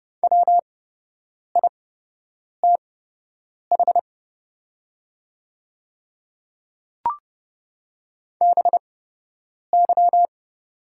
Morse code, heard as a keyed tone, about 30 wpm, spaced as for 8 wpm, 700 Hz. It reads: WITH BY
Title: 8x Character Spacing - Top 100 Words - 30wpm